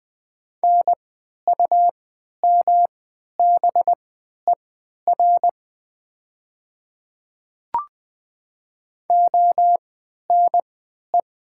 Morse code, heard as a keyed tone, about 20 wpm, spaced as for 11 wpm, 700 Hz. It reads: NUMBER ONE
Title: 3x Character Spacing - Top 100 Words - 20wpm